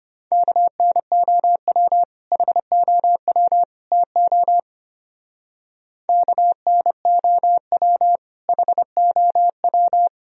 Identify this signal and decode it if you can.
Morse code, heard as a keyed tone, about 32 wpm, 700 Hz. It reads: KNOW HOW TO KNOW HOW